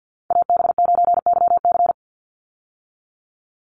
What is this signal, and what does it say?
Morse code, heard as a keyed tone, about 48 wpm, 700 Hz. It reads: AB9YC
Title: Call Signs Repeated 2x - 50wpm